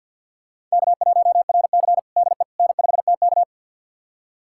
Morse code, heard as a keyed tone, about 53 wpm, 700 Hz. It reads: K1RC DE N5TK
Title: USA Callsign DE Callsign - 4 Characters - 50wpm